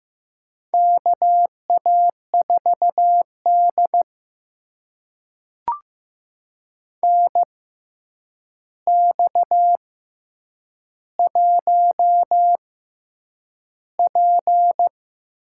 Morse code, heard as a keyed tone, about 15 wpm, spaced as for 8 wpm, 700 Hz. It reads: KA4D NX1P
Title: USA Call Signs - 4 Characters - 6x Character Spacing - 15wpm